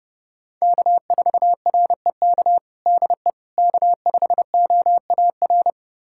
Morse code, heard as a keyed tone, about 32 wpm, 700 Hz. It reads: K4REK DE K5OAR
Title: USA Callsign DE Callsign - 5 Characters - 30wpm